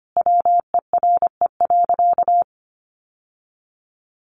Morse code, heard as a keyed tone, about 25 wpm, 700 Hz. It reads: WERE.
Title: Sentences from Top 900 Words 25wpm